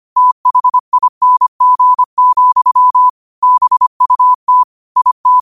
Morse code, heard as a keyed tone, about 25 wpm, 1000 Hz. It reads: THING, BUT IT
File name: qotdfiles / 25wpm